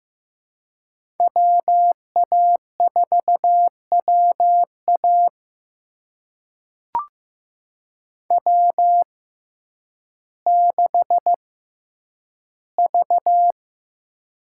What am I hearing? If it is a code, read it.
Morse code, heard as a keyed tone, about 15 wpm, spaced as for 9 wpm, 700 Hz. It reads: WA4WA W6V
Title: USA Call Signs - 5 Characters - 6x Character Spacing - 15wpm